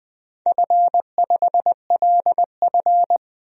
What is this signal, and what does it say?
Morse code, heard as a keyed tone, about 20 wpm, 700 Hz. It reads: F5LF
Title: DXCC-100 Call Signs - 4 Characters - 20wpm